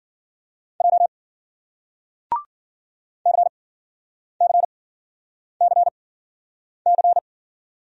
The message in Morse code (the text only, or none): J CCCC